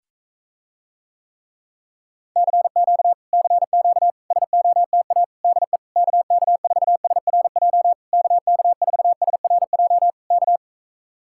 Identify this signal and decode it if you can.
Morse code, heard as a keyed tone, about 45 wpm, 700 Hz. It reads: CQ CQ SOTA DE KK4SRJ KK4SRJ K